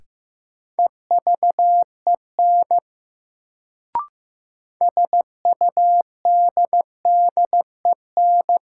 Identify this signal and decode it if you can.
Morse code, heard as a keyed tone, about 15 wpm, 700 Hz. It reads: EVEN SUDDEN
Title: Sets of 4 Words - New Word Drill - 15wpm